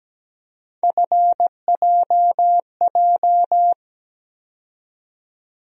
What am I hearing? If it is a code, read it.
Morse code, heard as a keyed tone, about 17 wpm, 700 Hz. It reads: FJJ
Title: USA Call Sign Postfixes - 3 Letters 17wpm